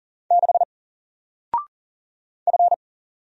Morse code, heard as a keyed tone, about 40 wpm, 700 Hz. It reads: 6 F
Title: Single Letter-Number 40wpm